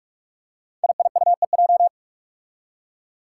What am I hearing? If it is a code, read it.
Morse code, heard as a keyed tone, about 46 wpm, 700 Hz. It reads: IIUEJ
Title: Five Challenging Letter-Number Combinations 45wpm